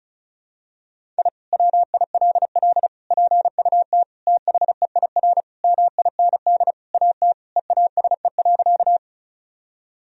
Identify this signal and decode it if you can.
Morse code, heard as a keyed tone, about 36 wpm, 700 Hz. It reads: I WILL PUT THEIR MIND AT EASE.
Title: Sentences from Top 1000 Words - Review Part 1 - 35wpm